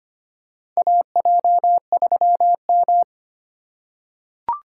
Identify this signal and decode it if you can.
Morse code, heard as a keyed tone, about 25 wpm, 700 Hz. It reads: AJ3M
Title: USA Call Signs - 4 Characters - 2x Character Spacing - 25wpm